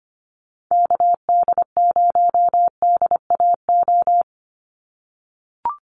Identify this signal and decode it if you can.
Morse code, heard as a keyed tone, about 25 wpm, 700 Hz. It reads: KD0DAO
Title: USA Call Signs 25wpm